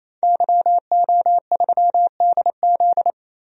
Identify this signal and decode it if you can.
Morse code, heard as a keyed tone, about 28 wpm, 700 Hz. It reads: YO3DZ
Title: DXCC-100 Call Signs - 5 Characters - 28wpm